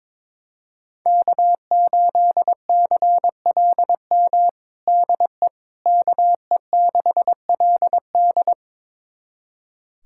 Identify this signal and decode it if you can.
Morse code, heard as a keyed tone, about 22 wpm, 700 Hz. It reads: K8CLM DE KE6LD
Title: USA Callsign DE Callsign - 5 Characters - 22wpm